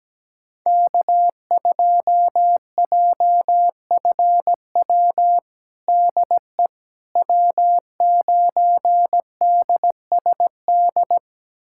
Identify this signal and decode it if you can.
Morse code, heard as a keyed tone, about 17 wpm, 700 Hz. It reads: K2JFW DE W9DSD
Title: USA Callsign DE Callsign - 5 Characters - 17wpm